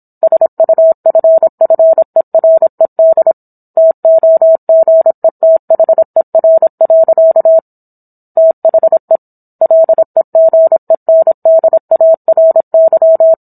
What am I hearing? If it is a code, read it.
Morse code, heard as a keyed tone, about 27 wpm, 650 Hz. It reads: SUFFERED TOGETHER. THE LEGENDARY
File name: Book One / 17-Chapter 17 - 26 WPM